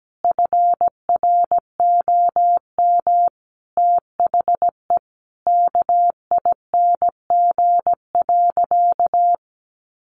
Morse code, heard as a keyed tone, about 17 wpm, 700 Hz. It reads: FROM THE KING.
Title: Sentences from Top 1400 Words 17wpm